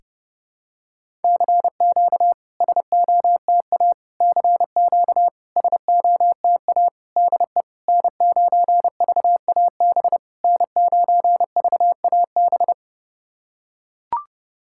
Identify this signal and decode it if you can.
Morse code, heard as a keyed tone, about 30 wpm, 690 Hz. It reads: CQ SOTA CQ SOTA DE N9VAB N9VAB
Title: SOTA QSO Practice 30wpm